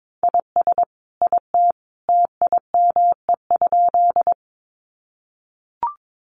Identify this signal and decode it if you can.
Morse code, heard as a keyed tone, about 22 wpm, 700 Hz. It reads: IS IT TIME?
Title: Sentences from Top 100 Words 22wpm